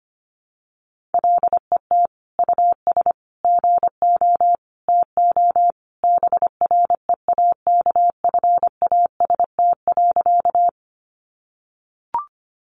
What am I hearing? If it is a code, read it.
Morse code, heard as a keyed tone, about 25 wpm, 700 Hz. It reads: LET US GO TO BREAKFAST.
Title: Sentences from Top 1500 Words 25wpm